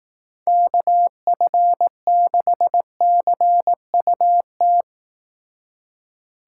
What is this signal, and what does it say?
Morse code, heard as a keyed tone, about 18 wpm, 700 Hz. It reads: KF6CUT
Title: Speed Racing - USA 2x3 Call Signs 15wpm